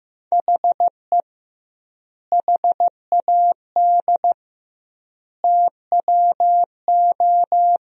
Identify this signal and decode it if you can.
Morse code, heard as a keyed tone, about 15 wpm, 700 Hz. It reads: HE HAD TWO
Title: 2x Word-Spacing - Sentences from Top 300 Words - 15wpm